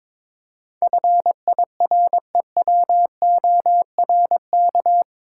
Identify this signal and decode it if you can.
Morse code, heard as a keyed tone, about 22 wpm, 700 Hz. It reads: FIREWORK
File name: practice / 22wpm / Eight Letter Words - 22wpm